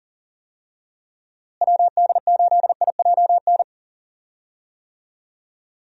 Morse code, heard as a keyed tone, about 40 wpm, 700 Hz. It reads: WD8IJN
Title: USA Call Signs - 6 Characters - Repeated 2x - 40wpm